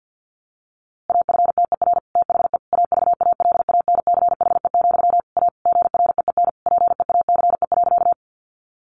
Morse code, heard as a keyed tone, about 49 wpm, 700 Hz. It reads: AFTER THE AVALANCHE, A GREEN PEACE.
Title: A Sound Of Thunder 50wpm